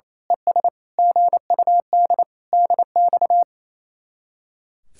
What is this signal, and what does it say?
Morse code, heard as a keyed tone, about 27 wpm, 700 Hz. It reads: ES GUD DX
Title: Speed-Racing Common QSO Phrases Encore - 28wpm